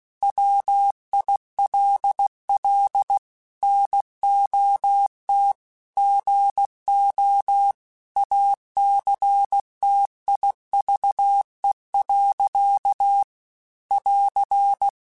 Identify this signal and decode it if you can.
Morse code, heard as a keyed tone, about 16 wpm, 800 Hz. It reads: WILL NOT GO ACTIVE. <AR>
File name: Technical Subjects / Technical Subjects-mp3s / OR Gate @15